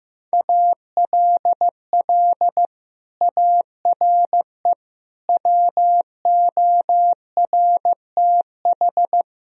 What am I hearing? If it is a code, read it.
Morse code, heard as a keyed tone, about 16 wpm, 700 Hz. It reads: ALL ARE WORTH
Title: Sentences from Top 1000 Words - Review Part 3 - 15wpm